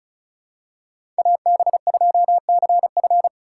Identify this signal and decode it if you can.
Morse code, heard as a keyed tone, about 34 wpm, 690 Hz. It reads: AB2CF
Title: USA Call Signs - 5 Characters - 2x Character Spacing - 35wpm